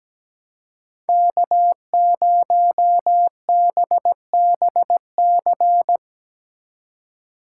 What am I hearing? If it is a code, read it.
Morse code, heard as a keyed tone, about 17 wpm, 700 Hz. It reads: K0BBC
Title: USA Call Signs - 3x Character Spacing - 17wpm